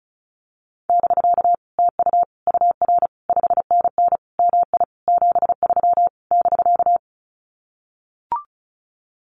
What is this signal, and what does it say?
Morse code, heard as a keyed tone, about 34 wpm, 700 Hz. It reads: <BK> TU UR 5NN MI 73 <BK>
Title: Common QSO Phrases Encore 35wpm